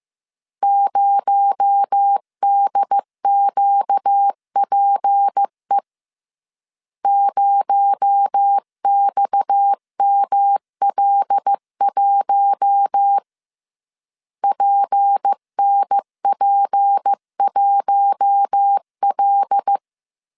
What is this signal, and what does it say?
Morse code, heard as a keyed tone, about 15 wpm, 800 Hz. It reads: 0DQPE 0XML1 PNP1L
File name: CW / DET / 1 6 1500C S C 14wpm 2951